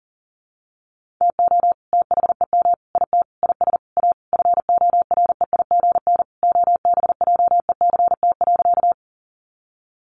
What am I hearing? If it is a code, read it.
Morse code, heard as a keyed tone, about 37 wpm, 700 Hz. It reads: TO THEM IT IS A FOREIGN OBJECT.
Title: Sentences from Top 1400 Words 40wpm